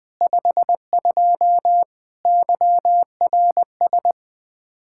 Morse code, heard as a keyed tone, about 20 wpm, 700 Hz. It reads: 52 YRS